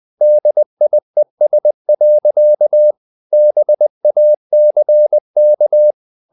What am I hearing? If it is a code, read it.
Morse code, heard as a keyed tone, about 20 wpm, 600 Hz. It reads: DIES. BACK
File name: the old man and the sea part 2 20wpm